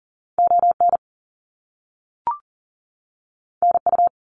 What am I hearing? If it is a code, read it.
Morse code, heard as a keyed tone, about 40 wpm, 700 Hz. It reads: ON NU
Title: Canadian Provinces and Territories 40wpm